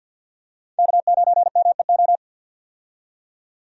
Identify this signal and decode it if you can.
Morse code, heard as a keyed tone, about 50 wpm, 700 Hz. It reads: K9MEO
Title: USA Call Signs - 5 Characters - 50wpm